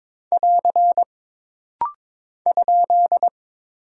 Morse code, heard as a keyed tone, about 22 wpm, 700 Hz. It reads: <AR> ?
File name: practice / 22wpm / Single Punctuation-Prosign - Rapid-Fire - 22wpm